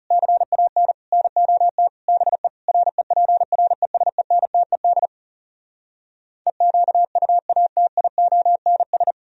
Morse code, heard as a keyed tone, about 40 wpm, 700 Hz. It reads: CAN NOT BE REPRESENTED EQUATIONS